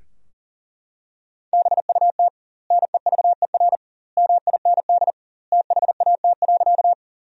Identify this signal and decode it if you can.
Morse code, heard as a keyed tone, about 37 wpm, 700 Hz. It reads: BUT NEVER MIND THAT.
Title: 2x Word-Spacing - Sentences from Top 500 Words - 40wpm